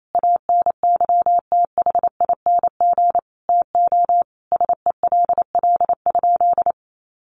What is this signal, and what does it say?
Morse code, heard as a keyed tone, about 28 wpm, 700 Hz. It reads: ANYTHING TO SELL?
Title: Sentences from Top 1100 Words 28wpm